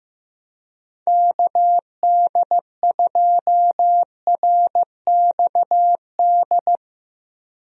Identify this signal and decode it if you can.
Morse code, heard as a keyed tone, about 15 wpm, 700 Hz. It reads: KD2RXD
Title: USA Call Signs - 6 Characters - Repeated 2x - 15wpm